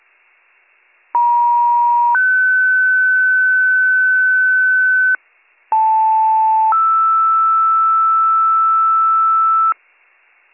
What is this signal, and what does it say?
Two-tone page, heard as a one-second tone, then a three-second tone, about 950 then 1500 Hz, about 880 then 1320 Hz. It is heard from 1.1 to 9.7 s.